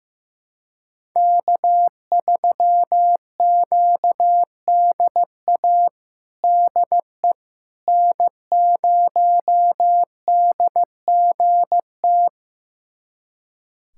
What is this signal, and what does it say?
Morse code, heard as a keyed tone, about 15 wpm, 700 Hz. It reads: K3QDA DE N0DGT
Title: USA Callsign DE Callsign - 5 Characters - 15wpm